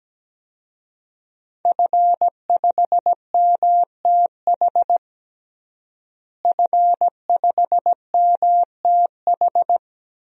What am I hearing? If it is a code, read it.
Morse code, heard as a keyed tone, about 17 wpm, 700 Hz. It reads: F5MTH F5MTH
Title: DXCC-100 Call Signs - 5 Characters - Repeated 2x - 17wpm